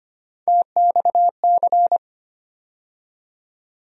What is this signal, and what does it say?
Morse code, heard as a keyed tone, about 25 wpm, 700 Hz. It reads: TXC
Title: USA Call Sign Postfixes - 3 Letters 25wpm